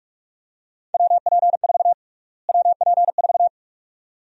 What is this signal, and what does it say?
Morse code, heard as a keyed tone, about 43 wpm, 700 Hz. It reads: WP4 WP4
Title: USA Call Sign Prefixes - Repeated 2x - 45wpm